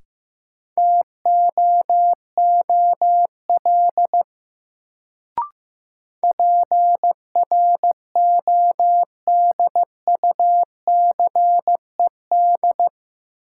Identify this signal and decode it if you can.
Morse code, heard as a keyed tone, about 16 wpm, 700 Hz. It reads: TOOL PRODUCED
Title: Sentences from Top 1000 Words - Review Part 6 - New Word Drill - 15wpm